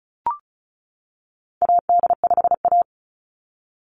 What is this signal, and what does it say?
Morse code, heard as a keyed tone, about 34 wpm, 700 Hz. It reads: AD5A
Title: NA SOTA Activator Callsigns 35wpm